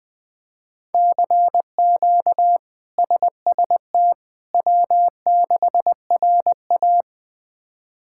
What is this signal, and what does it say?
Morse code, heard as a keyed tone, about 21 wpm, 700 Hz. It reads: CQ SST W6RA
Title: K1USN SST QSO Practice - 20wpm